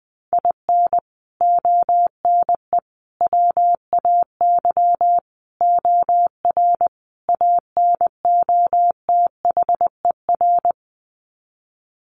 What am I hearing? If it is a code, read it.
Morse code, heard as a keyed tone, about 20 wpm, 700 Hz. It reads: IN ONE WAY OR ANOTHER
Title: Sets of 5 Words Encore 20wpm